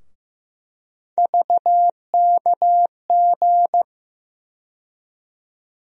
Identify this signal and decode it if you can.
Morse code, heard as a keyed tone, about 15 wpm, 700 Hz. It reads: VKG
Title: USA Call Sign Postfixes - 3 Letters 15wpm